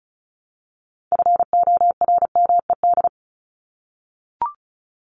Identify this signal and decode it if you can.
Morse code, heard as a keyed tone, about 34 wpm, 700 Hz. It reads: FORMED
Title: ED Suffix - Top 100 Words with Base Words 35wpm